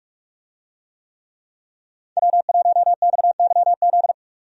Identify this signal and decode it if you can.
Morse code, heard as a keyed tone, about 46 wpm, 700 Hz. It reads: W1XYZ
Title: USA Call Signs - 5 Characters - 45wpm